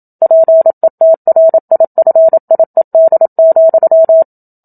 Morse code, heard as a keyed tone, about 27 wpm, 650 Hz. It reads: PETRIFIED,
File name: Book Two / 01-Chapter 1 - 27 WPM